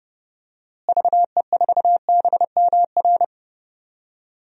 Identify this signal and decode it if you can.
Morse code, heard as a keyed tone, about 30 wpm, 700 Hz. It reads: VE4BMR
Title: DXCC-100 Call Signs - 6 Characters - Repeated 3x - 30wpm